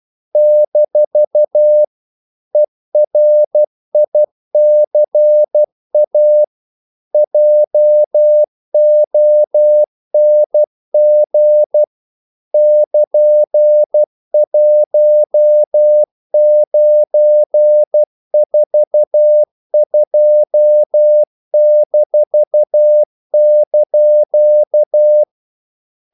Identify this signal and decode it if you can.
Morse code, heard as a keyed tone, about 12 wpm, 600 Hz. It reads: - ERICA JONG <KN>1942-)
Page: English quotations in CW (telegraphy - morse)